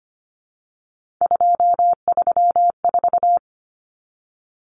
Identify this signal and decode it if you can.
Morse code, heard as a keyed tone, about 24 wpm, 700 Hz. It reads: 234
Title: Numbers - 3-Digit 25wpm